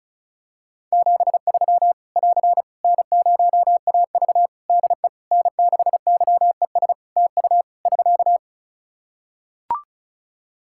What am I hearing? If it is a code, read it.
Morse code, heard as a keyed tone, about 36 wpm, 700 Hz. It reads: 73 <AR> N0AV DE N6YES TU <SK>
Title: Practice QSO - Exchange 4 - Ending - 35wpm